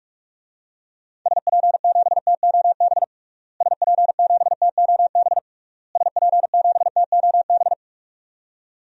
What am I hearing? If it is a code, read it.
Morse code, heard as a keyed tone, about 46 wpm, 700 Hz. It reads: SP7TOB SP7TOB SP7TOB
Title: DXCC-100 Call Signs - 6 Characters - Repeated 3x - 45wpm